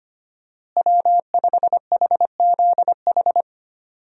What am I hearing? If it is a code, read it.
Morse code, heard as a keyed tone, about 25 wpm, 700 Hz. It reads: W5HZH
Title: USA Call Signs - 5 Characters - Repeated 3x - 25wpm